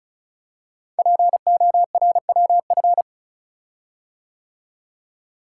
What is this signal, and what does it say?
Morse code, heard as a keyed tone, about 36 wpm, 690 Hz. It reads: PORWF
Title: Code Groups - 5 Letters - 35wpm